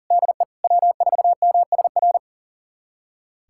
Morse code, heard as a keyed tone, about 40 wpm, 700 Hz. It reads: DE W4MSR
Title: USA Callsign DE Callsign - 5 Characters - 40wpm